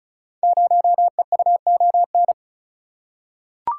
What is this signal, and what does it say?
Morse code, heard as a keyed tone, about 36 wpm, 700 Hz. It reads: 0EUON
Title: Code Groups - 5 Letter:Numbers - 35wpm